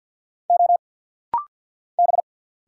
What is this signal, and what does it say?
Morse code, heard as a keyed tone, about 50 wpm, 700 Hz. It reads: O B
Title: Single Letter-Number Alt - Rapid-Fire 50wpm